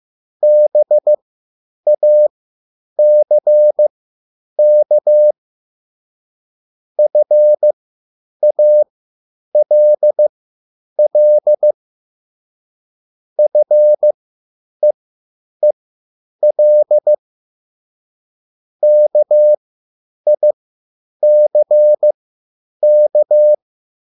Morse code, heard as a keyed tone, about 15 wpm, spaced as for 9 wpm, 600 Hz. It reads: BACK FALL FEEL KICK